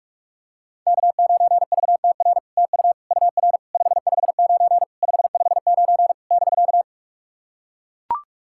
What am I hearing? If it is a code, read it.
Morse code, heard as a keyed tone, about 44 wpm, 700 Hz. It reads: K9VTR TU UR 559 559 <BK>
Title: SOTA QSO Practice 45wpm